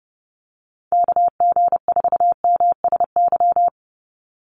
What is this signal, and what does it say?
Morse code, heard as a keyed tone, about 30 wpm, 700 Hz. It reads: KG4MSY
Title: USA Call Signs 30wpm